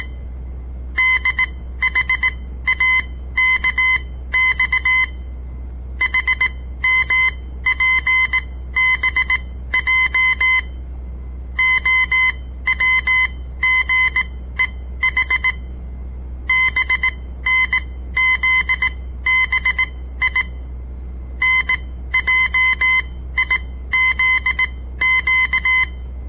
Morse code, heard as a keyed tone, about 18 wpm, spaced as for 14 wpm, 2050 Hz. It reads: DHAKX HMPBJ OWGEH BNZBI NJIZQ
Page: On the air Morse code practice from the Southern Montana Amatuer Radio Association; all code, no read back